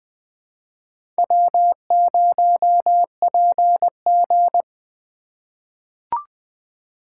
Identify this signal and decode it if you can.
Morse code, heard as a keyed tone, about 20 wpm, 700 Hz. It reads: W0PG
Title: USA Call Signs - 2x Character Spacing - 20wpm